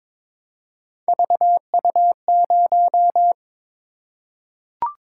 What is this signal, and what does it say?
Morse code, heard as a keyed tone, about 22 wpm, 700 Hz. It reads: VU0
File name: practice / 22wpm / Three Letter-Number Combinations - 22wpm